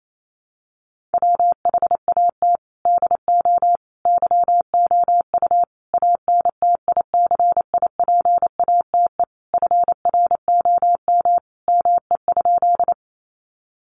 Morse code, heard as a keyed tone, about 28 wpm, 700 Hz. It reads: WHAT DO YOU ANTICIPATE FROM ME?
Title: Sentences from Top 1500 Words 28wpm